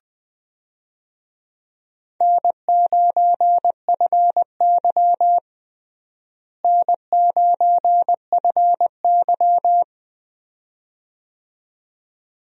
Morse code, heard as a keyed tone, about 20 wpm, 700 Hz. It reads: N9FY N9FY
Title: USA Call Signs - 4 Characters - Repeated 2x - 20wpm